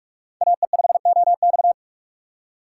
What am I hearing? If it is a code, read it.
Morse code, heard as a keyed tone, about 46 wpm, 700 Hz. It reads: AE5OX
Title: USA Call Signs - 6x Character Spacing - 45wpm